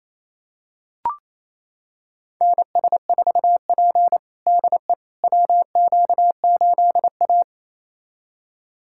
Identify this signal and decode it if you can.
Morse code, heard as a keyed tone, about 28 wpm, 700 Hz. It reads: NS4P DE WQ8A
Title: USA Callsign DE Callsign - 4 Characters - 28wpm